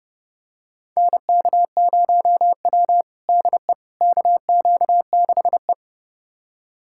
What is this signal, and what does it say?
Morse code, heard as a keyed tone, about 30 wpm, 700 Hz. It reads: NK0W DE KQ6E
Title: USA Callsign DE Callsign - 4 Characters - 30wpm